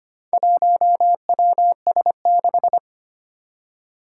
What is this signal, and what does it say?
Morse code, heard as a keyed tone, about 25 wpm, 700 Hz. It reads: 1WS6
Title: Four Challenging Letter-Number Combinations 25wpm